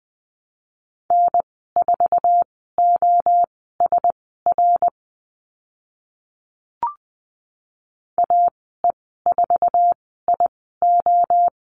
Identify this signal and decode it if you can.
Morse code, heard as a keyed tone, about 20 wpm, spaced as for 14 wpm, 700 Hz. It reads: N4OSR AE4IO
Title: 055 Lesson 10g - Letter r - Call Signs 20wpm